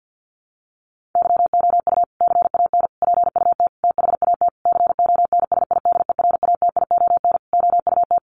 Morse code, heard as a keyed tone, about 50 wpm, 700 Hz. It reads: YOU CAN PUT THAT CONSIDERATION OUT
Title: Sentences from Top 1500 Words 50wpm